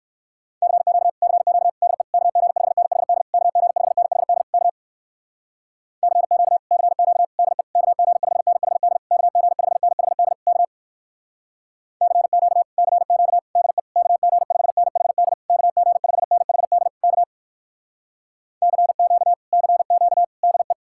Morse code, heard as a keyed tone, about 66 wpm, 700 Hz. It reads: CQ CQ DE KG5NHD KG5NHD K CQ CQ DE KG5NHD KG5NHD K CQ CQ DE KG5NHD KG5NHD K CQ CQ DE